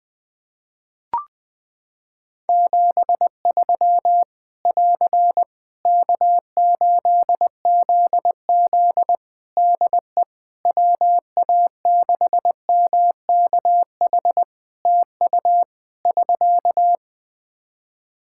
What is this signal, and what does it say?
Morse code, heard as a keyed tone, about 20 wpm, 700 Hz. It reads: 73 <AR> K8ZZ DE WA6MKH TU <SK>
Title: K1USN SST QSO Practice 20wpm